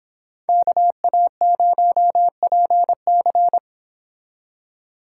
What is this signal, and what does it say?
Morse code, heard as a keyed tone, about 25 wpm, 700 Hz. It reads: KA0PC